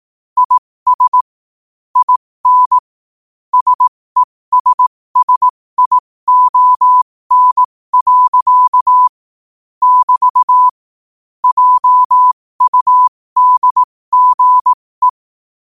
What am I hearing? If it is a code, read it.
Morse code, heard as a keyed tone, about 18 wpm, spaced as for 15 wpm, 1000 Hz. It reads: IS IN SESSION. <BT> JUDGE
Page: 15 WPM morse code quotes